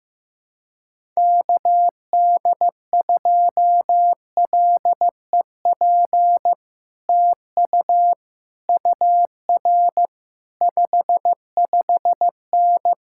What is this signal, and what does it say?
Morse code, heard as a keyed tone, about 15 wpm, 700 Hz. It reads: KD2LEP TU UR 55N
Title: Speed-Racing Common QSO Phrases Encore - 15wpm